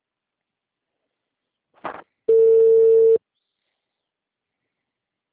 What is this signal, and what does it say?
Ringback tone (China, 450 Hz).